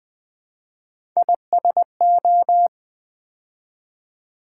Morse code, heard as a keyed tone, about 20 wpm, 700 Hz. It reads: ISO
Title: Build A Word - 3 Letters At A Time - 9-Letter Words - 20wpm